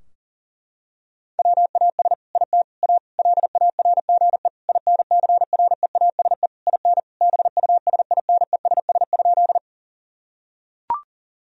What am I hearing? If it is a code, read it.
Morse code, heard as a keyed tone, about 37 wpm, 700 Hz. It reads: WAS IT A LARGE INCREASE IN BUSINESS?
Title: Sentences from Top 1000 Words - Review Part 4 - 40wpm